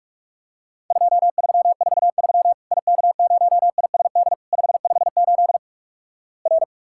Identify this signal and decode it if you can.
Morse code, heard as a keyed tone, about 48 wpm, 700 Hz. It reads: 2343 IK0ISD 558 R
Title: Worked All Europe DX Contest - QTC Exchange - 45wpm